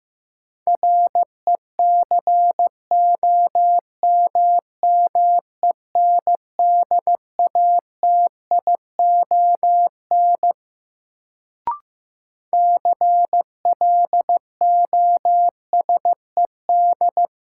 Morse code, heard as a keyed tone, about 15 wpm, 700 Hz. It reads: RECOMMENDATION CLOSED
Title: Sentences from Top 1400 Words - New Word Drill Alt - 15wpm